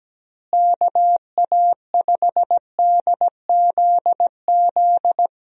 Morse code, heard as a keyed tone, about 17 wpm, 700 Hz. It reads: KA5DZZ
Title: USA Call Signs - 6 Characters - Repeated 2x - 17wpm